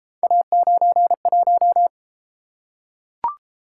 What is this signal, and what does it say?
Morse code, heard as a keyed tone, about 31 wpm, 700 Hz. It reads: A91